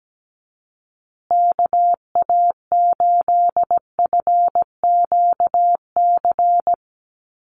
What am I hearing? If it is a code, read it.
Morse code, heard as a keyed tone, about 17 wpm, 700 Hz. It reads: KA8FQC
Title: USA Call Signs Repeated 3x 17wpm